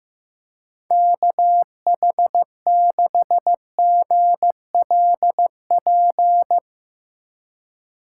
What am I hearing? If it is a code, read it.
Morse code, heard as a keyed tone, about 16 wpm, 700 Hz. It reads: KH6GLP
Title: Speed Racing - USA 2x3 Call Signs 15wpm